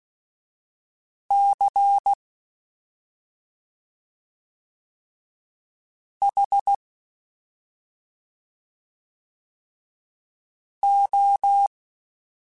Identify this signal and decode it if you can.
Morse code, heard as a keyed tone, about 16 wpm, spaced as for 2 wpm, 800 Hz. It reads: CHO
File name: Special INGs-mp3s / Special INGs-mp3s / Slow ing @10